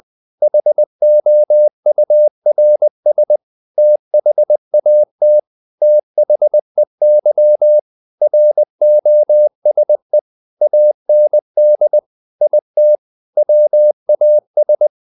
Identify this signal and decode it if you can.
Morse code, heard as a keyed tone, about 20 wpm, 600 Hz. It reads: HOURS THAT THEY ROSE AND IT WAS
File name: the old man and the sea / the old man and the sea part 1 20wpm